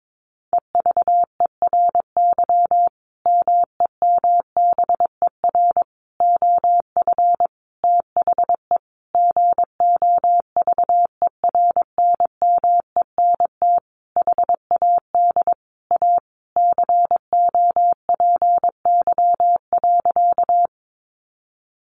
Morse code, heard as a keyed tone, about 22 wpm, 700 Hz. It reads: EVERY MEMBER OF THE GOVERNMENT HAD A COPY.
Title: Sentences from Top 1400 Words 22wpm